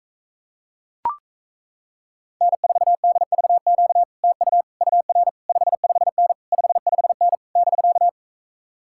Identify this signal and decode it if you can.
Morse code, heard as a keyed tone, about 37 wpm, 700 Hz. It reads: N4DVQ TU UR 55N 55N <BK>